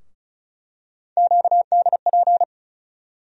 Morse code, heard as a keyed tone, about 36 wpm, 700 Hz. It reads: QDP
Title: USA Call Sign Postfixes - 3 Letters 35wpm